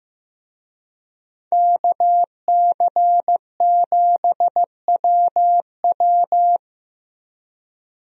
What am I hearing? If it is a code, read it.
Morse code, heard as a keyed tone, about 16 wpm, 700 Hz. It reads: KC7WW
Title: USA Call Signs - 5 Characters - Repeated 3x - 15wpm